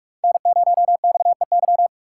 Morse code, heard as a keyed tone, about 46 wpm, 700 Hz. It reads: N0XEY
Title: USA Call Signs - 5 Characters - 2x Character Spacing - 45wpm